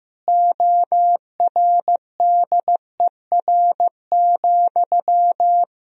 Morse code, heard as a keyed tone, about 15 wpm, 700 Hz. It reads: ORDER,
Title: Sentences from Top 1000 Words - Review Part 2 - 15wpm